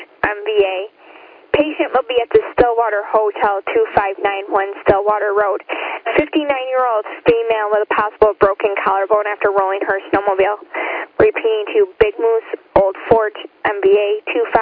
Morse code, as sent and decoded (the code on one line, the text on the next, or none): none